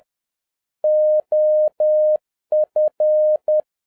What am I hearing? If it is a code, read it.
Morse code, heard as a keyed tone, about 10 wpm, 610 Hz. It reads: OF
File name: 2011-07-mp3 / n21 GREETINGS FROM SINGAPORE 10wpm